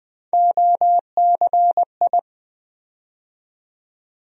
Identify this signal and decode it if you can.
Morse code, heard as a keyed tone, about 20 wpm, 700 Hz. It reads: OCI
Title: USA Call Sign Postfixes - 3 Letters 20wpm